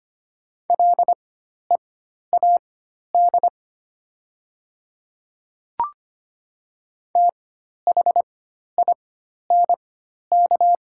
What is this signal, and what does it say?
Morse code, heard as a keyed tone, about 25 wpm, spaced as for 12 wpm, 700 Hz. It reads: LEAD THINK
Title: 4x Character Spacing - Top 500 Words - 25wpm